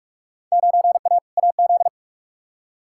Morse code, heard as a keyed tone, about 46 wpm, 700 Hz. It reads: 9A AZ
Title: Field Day Exchanges - 45wpm